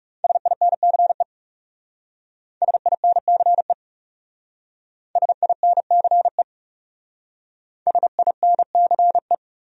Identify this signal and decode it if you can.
Morse code, heard as a keyed tone, about 36 wpm, 700 Hz. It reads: SINCE SINCE SINCE SINCE
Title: Speed-Racing Top 300 Words 30wpm